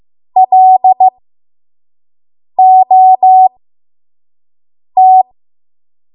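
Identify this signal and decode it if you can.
Morse code, heard as a keyed tone, about 15 wpm, spaced as for 5 wpm, 750 Hz. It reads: LOT
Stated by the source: 5 WPM Morse code practice.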